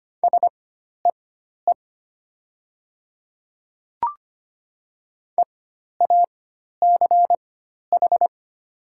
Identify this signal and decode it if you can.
Morse code, heard as a keyed tone, about 25 wpm, spaced as for 12 wpm, 700 Hz. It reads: SEE EACH
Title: 4x Character Spacing - Top 100 Words - 25wpm